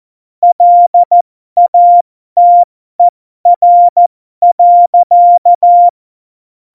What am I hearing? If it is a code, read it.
Morse code, heard as a keyed tone, about 14 wpm, spaced as for 12 wpm, 700 Hz. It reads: LATER.
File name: Morsekurs / Witze